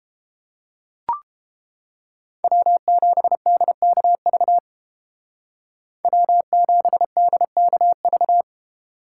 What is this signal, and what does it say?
Morse code, heard as a keyed tone, about 30 wpm, 700 Hz. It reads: W7DKV W7DKV